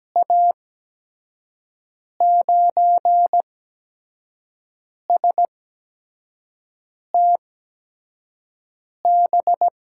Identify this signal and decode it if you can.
Morse code, heard as a keyed tone, about 17 wpm, spaced as for 5 wpm, 700 Hz. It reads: A9STB
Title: USA Call Signs - 6 Characters - 8x Character Spacing - 17wpm